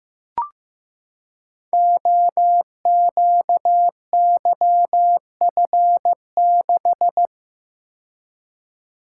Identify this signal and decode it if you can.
Morse code, heard as a keyed tone, about 15 wpm, 700 Hz. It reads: OQYF6
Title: Code Groups - 5 Letter:Numbers - 15wpm